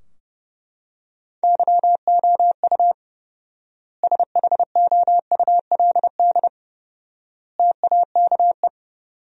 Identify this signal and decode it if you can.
Morse code, heard as a keyed tone, about 31 wpm, 700 Hz. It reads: YOU SHOULD TAKE
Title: 4x Word-Spacing - Sentences from Top 300 Words - 30wpm